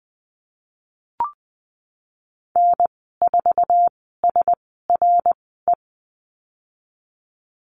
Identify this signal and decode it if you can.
Morse code, heard as a keyed tone, about 20 wpm, spaced as for 15 wpm, 700 Hz. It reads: N4SRE
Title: 088 Lesson 15g - Number 5 - Call Signs 20wpm